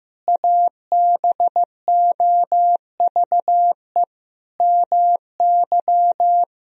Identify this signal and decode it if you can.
Morse code, heard as a keyed tone, about 14 wpm, 700 Hz. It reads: ABOVE MY
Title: Sentences from Top 1000 Words - Review Part 3 - 15wpm